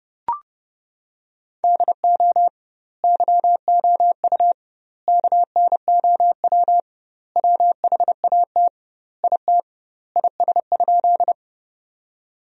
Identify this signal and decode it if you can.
Morse code, heard as a keyed tone, about 30 wpm, 700 Hz. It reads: DO YOU KNOW WHAT IT IS?
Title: 2x Word-Spacing - Sentences from Top 100 Words - 30wpm